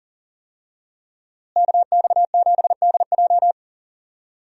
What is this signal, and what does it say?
Morse code, heard as a keyed tone, about 37 wpm, 700 Hz. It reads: KX7DJ
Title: USA Call Signs - 5 Characters - Repeated 3x - 40wpm